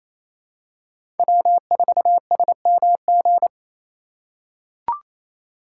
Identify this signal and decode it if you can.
Morse code, heard as a keyed tone, about 28 wpm, 700 Hz. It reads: W4SMG
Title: USA Call Signs - 5 Characters - 4x Character Spacing - 28wpm